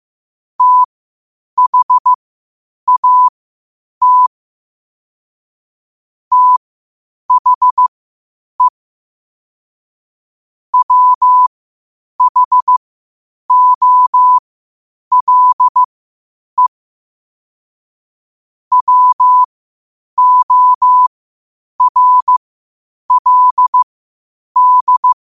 Morse code, heard as a keyed tone, about 15 wpm, spaced as for 8 wpm, 1000 Hz. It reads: THAT THE WHOLE WORLD
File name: qotdfiles / 8wpm